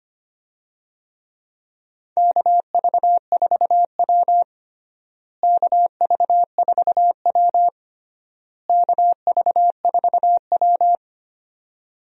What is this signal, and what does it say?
Morse code, heard as a keyed tone, about 25 wpm, 700 Hz. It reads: KV4W KV4W KV4W
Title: USA Call Signs - 4 Characters - Repeated 3x - 25wpm